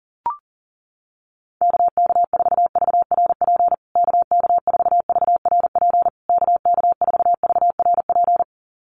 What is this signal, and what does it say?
Morse code, heard as a keyed tone, about 40 wpm, 700 Hz. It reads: KK4VRP KK4VRP KK4VRP
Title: USA Call Signs Repeated 3x 40wpm